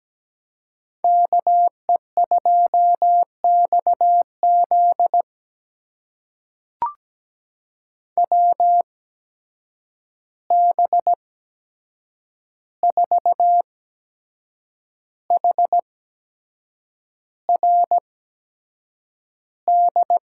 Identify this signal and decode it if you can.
Morse code, heard as a keyed tone, about 17 wpm, spaced as for 7 wpm, 700 Hz. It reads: KE2XZ WB4HRD
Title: USA Call Signs - 8x Character Spacing - 17wpm